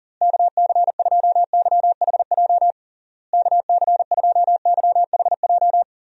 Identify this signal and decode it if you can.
Morse code, heard as a keyed tone, about 40 wpm, 700 Hz. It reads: KC2YHJ KC2YHJ
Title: USA Call Signs - 6 Characters - Repeated 3x - 40wpm